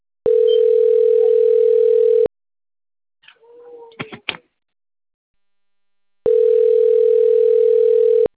Ringback tone (North America), 440 + 480 Hz.